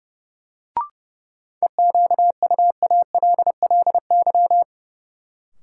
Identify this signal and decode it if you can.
Morse code, heard as a keyed tone, about 30 wpm, 700 Hz. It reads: EQUALLY
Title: Sentences from Top 1400 Words - New Word Drill Alt - 30wpm